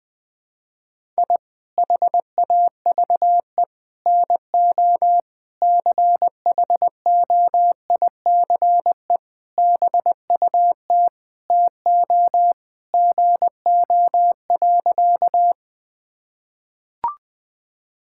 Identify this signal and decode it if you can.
Morse code, heard as a keyed tone, about 20 wpm, 700 Hz. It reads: I HAVE NO CHOICE BUT TO GO.
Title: Sentences from Top 1000 Words - Review Part 6 - 20wpm